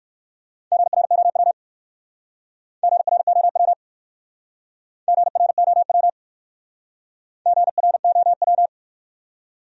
Morse code, heard as a keyed tone, about 56 wpm, 700 Hz. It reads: GROW GROW GROW GROW
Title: Speed-Racing Top 300 Words 45wpm